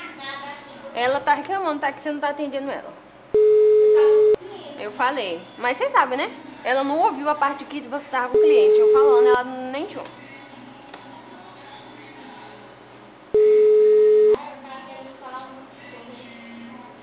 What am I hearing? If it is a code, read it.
Ringback tone (Europe, 425 Hz), with speech.
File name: out- -2062- - - .45602